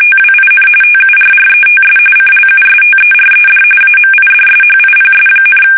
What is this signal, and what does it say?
RTTY, about 45.45 baud. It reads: TWIN TURRETS CLUSTERED AMIDSHIPS,
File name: fsk850 / USS-Massachusetts